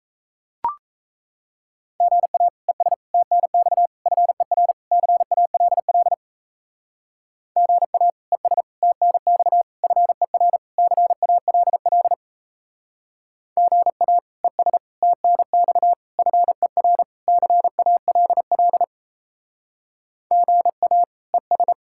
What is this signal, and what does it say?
Morse code, heard as a keyed tone, about 34 wpm, 700 Hz. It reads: GA ES TNX FER CALL GA ES TNX FER CALL GA ES TNX FER CALL GA ES
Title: Speed-Racing Common QSO Phrases - 28wpm